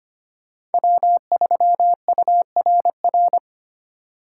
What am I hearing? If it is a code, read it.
Morse code, heard as a keyed tone, about 32 wpm, 700 Hz. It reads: W3URR